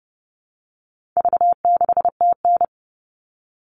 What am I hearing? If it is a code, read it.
Morse code, heard as a keyed tone, about 30 wpm, 700 Hz. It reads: V6TN
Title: All Cut Numbers - 4-Digit 30wpm